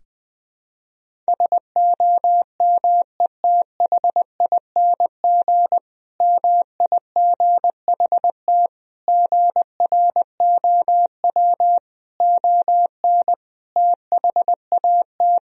Morse code, heard as a keyed tone, about 21 wpm, 700 Hz. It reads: SOMETHING MIGHT GROW ON THAT
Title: Sentences from Top 1000 Words - Review Part 1 - 20wpm